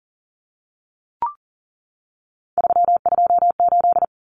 Morse code, heard as a keyed tone, about 40 wpm, 700 Hz. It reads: 328
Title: Common Cut Numbers - 3-Digit 40wpm